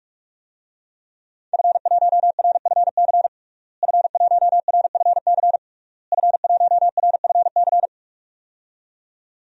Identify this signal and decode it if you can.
Morse code, heard as a keyed tone, about 41 wpm, 700 Hz. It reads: F1RFC F1RFC F1RFC
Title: DXCC-100 Call Signs - 5 Characters - Repeated 3x - 45wpm